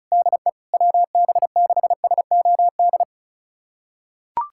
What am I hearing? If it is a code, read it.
Morse code, heard as a keyed tone, about 36 wpm, 700 Hz. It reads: DE WB6SOD
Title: USA Callsign DE Callsign - 6 Characters - 35wpm